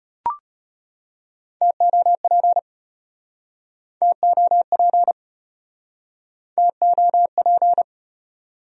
Morse code, heard as a keyed tone, about 35 wpm, 700 Hz. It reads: TOP TOP TOP